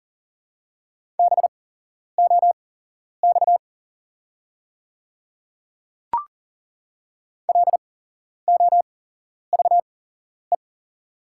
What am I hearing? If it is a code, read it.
Morse code, heard as a keyed tone, about 42 wpm, spaced as for 11 wpm, 690 Hz. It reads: BOX LOVE
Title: 8x Character Spacing - Top 500 Words - 40wpm